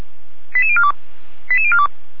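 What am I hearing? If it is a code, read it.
Five-tone selective call (ZVEI), heard as quick runs of five tones, tones of about 70 ms. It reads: 88042 88042
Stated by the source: ZVEI five-tone selective call received on a software-defined radio.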